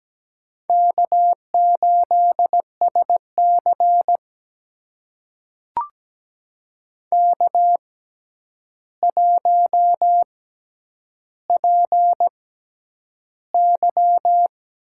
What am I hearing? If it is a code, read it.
Morse code, heard as a keyed tone, about 17 wpm, spaced as for 9 wpm, 700 Hz. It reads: K8SC K1PY
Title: USA Call Signs - 4 Characters - 6x Character Spacing - 17wpm